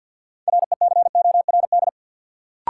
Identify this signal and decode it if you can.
Morse code, heard as a keyed tone, about 53 wpm, 700 Hz. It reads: RECORD
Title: Sentences from Top 500 Words - New Word Drill - 50wpm